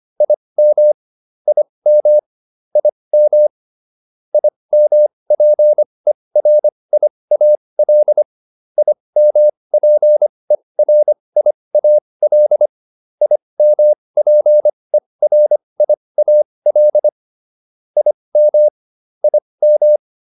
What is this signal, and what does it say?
Morse code, heard as a keyed tone, about 25 wpm, spaced as for 20 wpm, 600 Hz. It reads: IM IM IM IMPERIAL IMPERIAL IMPERIAL IM IM
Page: /advanced-practice-files/